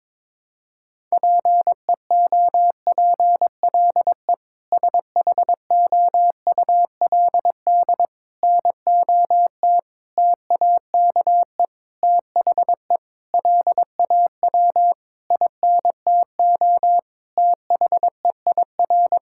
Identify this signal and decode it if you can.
Morse code, heard as a keyed tone, about 22 wpm, 700 Hz. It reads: PEOPLE SHOULD NOT TAKE THE LAW INTO THEIR
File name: practice / 22wpm / Sentences with 8-Word Phrases - 22wpm